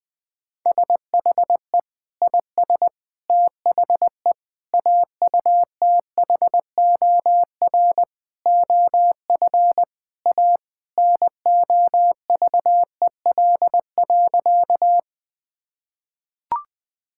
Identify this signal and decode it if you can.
Morse code, heard as a keyed tone, about 21 wpm, 700 Hz. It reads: SHE IS THE AUTHOR OF A NOVEL.
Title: Sentences with 4-Word Phrases - 20wpm